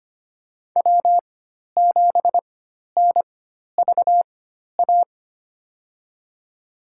Morse code, heard as a keyed tone, about 25 wpm, spaced as for 12 wpm, 700 Hz. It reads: W7NVA
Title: USA Call Signs - 4x Character Spacing - 25wpm